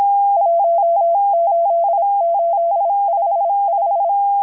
Morse code, heard as a keyed tone, about 26 wpm, 680 Hz. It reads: 18855
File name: HM02- -0437utc-7351khz